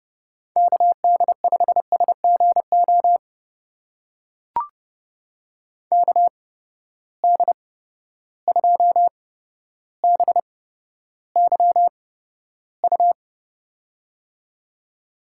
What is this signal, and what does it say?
Morse code, heard as a keyed tone, about 30 wpm, spaced as for 13 wpm, 700 Hz. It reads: KD5SGO KD2BYU
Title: USA Call Signs - 6 Characters - 8x Character Spacing - 30wpm